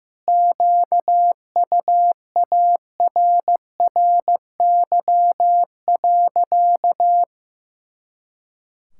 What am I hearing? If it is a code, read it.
Morse code, heard as a keyed tone, about 15 wpm, 700 Hz. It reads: QUARRY.
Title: Pangram Challenge v2 - 15wpm